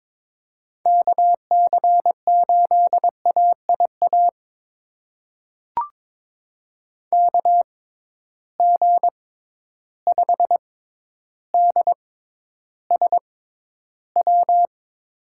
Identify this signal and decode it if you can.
Morse code, heard as a keyed tone, about 22 wpm, spaced as for 11 wpm, 700 Hz. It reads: KC8AIA KG5DSW
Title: USA Call Signs - 6 Characters - 6x Character Spacing - 22wpm